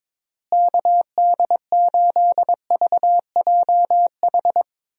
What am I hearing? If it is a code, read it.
Morse code, heard as a keyed tone, about 23 wpm, 700 Hz. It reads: KD8VJH
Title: USA Call Signs - 6 Characters - Repeated 2x - 22wpm